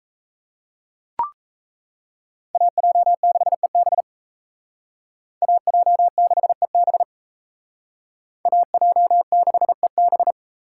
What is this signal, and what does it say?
Morse code, heard as a keyed tone, about 35 wpm, 700 Hz. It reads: AJ6EB AJ6EB AJ6EB
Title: Speed Racing - USA 2x2 Call Signs 28wpm